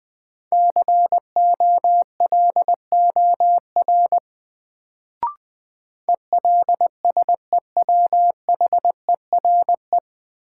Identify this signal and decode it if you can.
Morse code, heard as a keyed tone, about 20 wpm, 700 Hz. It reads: COLOR ELSEWHERE
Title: Sentences from Top 1600 Words - New Word Drill Alt - 20wpm